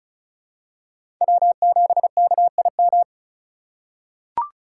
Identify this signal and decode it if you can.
Morse code, heard as a keyed tone, about 34 wpm, 700 Hz. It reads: W7KIM
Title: USA Call Signs - 5 Characters - 2x Character Spacing - 35wpm